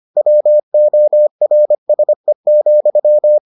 Morse code, heard as a keyed tone, about 25 wpm, 600 Hz. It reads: WORSE,